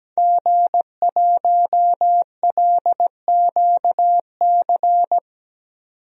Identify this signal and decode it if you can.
Morse code, heard as a keyed tone, about 17 wpm, 700 Hz. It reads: G1LQC
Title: DXCC-100 Call Signs - 5 Characters - 17wpm